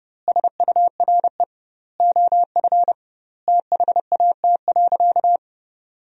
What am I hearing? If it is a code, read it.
Morse code, heard as a keyed tone, about 32 wpm, 700 Hz. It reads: SURE OF THAT.
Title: 2x Word-Spacing - Sentences from Top 300 Words - 30wpm